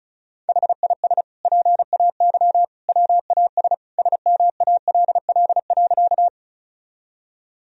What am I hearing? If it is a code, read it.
Morse code, heard as a keyed tone, about 36 wpm, 700 Hz. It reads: HIS PAY WAS SMALL.
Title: Sentences from Top 1000 Words - Review Part 1 - 35wpm